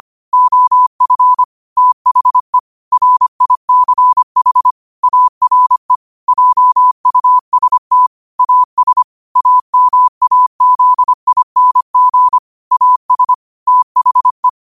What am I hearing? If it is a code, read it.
Morse code, heard as a keyed tone, about 25 wpm, 1000 Hz. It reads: OF THE RICH ARE JUST AS AMAZING AS THE
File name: 25wpm